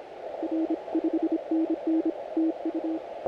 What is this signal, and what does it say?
Morse code, heard as a keyed tone, about 26 wpm, 320 Hz. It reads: R 5NN TU
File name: audio mp3 files / A71BX 40m CW 2008-4-12 2-0-53